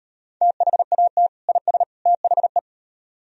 Morse code, heard as a keyed tone, about 36 wpm, 700 Hz. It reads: THAT IS THE